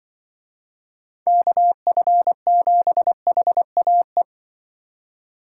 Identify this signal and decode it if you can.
Morse code, heard as a keyed tone, about 25 wpm, 700 Hz. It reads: KF7HAE
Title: Speed Racing - USA Call Signs 20wpm